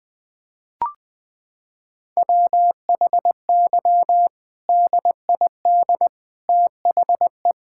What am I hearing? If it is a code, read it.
Morse code, heard as a keyed tone, about 20 wpm, 700 Hz. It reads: WHY DID THE
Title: Sentences from Top 1000 Words - Review Part 5 - 20wpm